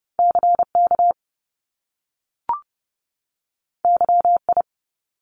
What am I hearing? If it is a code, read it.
Morse code, heard as a keyed tone, about 30 wpm, 700 Hz. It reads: CK YI
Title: Top Two Letter Combinations in Words 30wpm